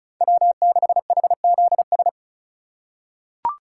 Morse code, heard as a keyed tone, about 36 wpm, 700 Hz. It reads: W6HZS
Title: USA Call Signs - 5 Characters - 4x Character Spacing - 35wpm